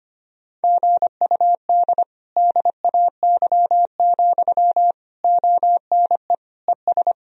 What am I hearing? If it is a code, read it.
Morse code, heard as a keyed tone, about 25 wpm, 700 Hz. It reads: GUD DAY, ONE ES
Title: Sentences from Top 200 Words and Abbreviations 25wpm